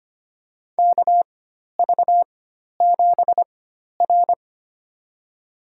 Morse code, heard as a keyed tone, about 25 wpm, spaced as for 12 wpm, 700 Hz. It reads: KV7R
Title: USA Call Signs - 4 Characters - 4x Character Spacing - 25wpm